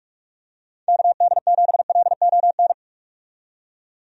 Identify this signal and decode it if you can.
Morse code, heard as a keyed tone, about 46 wpm, 700 Hz. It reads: KD7LON
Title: USA Call Signs - 6 Characters - 45wpm